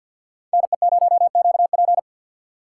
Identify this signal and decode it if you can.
Morse code, heard as a keyed tone, about 47 wpm, 700 Hz. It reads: NE0QP
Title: USA Call Signs - 4x Character Spacing - 50wpm